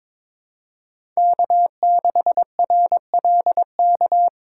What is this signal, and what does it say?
Morse code, heard as a keyed tone, about 22 wpm, 700 Hz. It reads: K6RLK
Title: USA Call Signs - 5 Characters - 4x Character Spacing - 22wpm